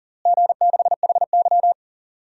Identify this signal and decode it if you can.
Morse code, heard as a keyed tone, about 40 wpm, 700 Hz. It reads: G6HY
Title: DXCC-100 Call Signs - 4 Characters - Repeated 2x - 40wpm